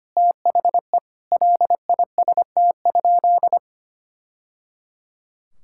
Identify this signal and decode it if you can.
Morse code, heard as a keyed tone, about 25 wpm, 700 Hz. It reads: THE LIST?
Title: Sentences from Top 1000 Words - Review Part 6 - 25wpm